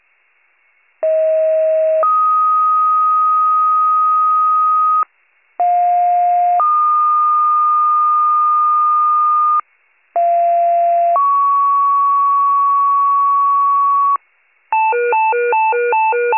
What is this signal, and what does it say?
Two-tone page, heard as a one-second tone, then a three-second tone, about 640 then 1250 Hz, about 710 then 1220 Hz, about 690 then 1120 Hz. It is heard from 1.0 to 14.2 s.